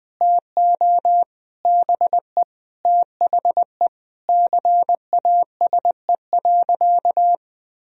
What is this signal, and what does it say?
Morse code, heard as a keyed tone, about 20 wpm, 700 Hz. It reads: TO BE THE CASE.
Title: Sentences from Top 1000 Words - Review Part 2 - 20wpm